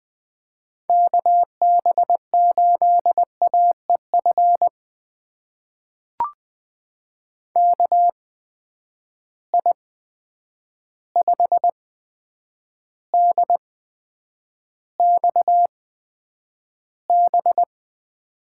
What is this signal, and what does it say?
Morse code, heard as a keyed tone, about 20 wpm, spaced as for 9 wpm, 700 Hz. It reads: KB8AEF KI5DXB
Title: USA Call Signs - 6 Characters - 8x Character Spacing - 20wpm